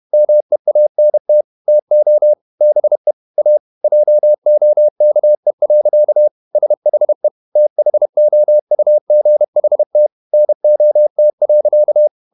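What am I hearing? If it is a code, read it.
Morse code, heard as a keyed tone, about 31 wpm, 600 Hz. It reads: MEANT TO BE A JOKE. SHE THOUGHT NOT.
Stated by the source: Morse code at 30 wpm, the novel Never by Ken Follett in English, from CW med Gnister.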